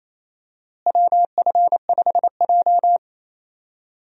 Morse code, heard as a keyed tone, about 28 wpm, 700 Hz. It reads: WF5J
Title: USA Call Signs - 4 Characters - Repeated 2x - 28wpm